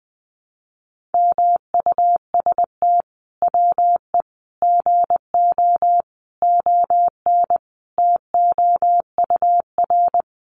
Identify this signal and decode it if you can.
Morse code, heard as a keyed tone, about 19 wpm, 700 Hz. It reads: MUST WE GO ON TOUR
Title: Sentences from Top 1400 Words 20wpm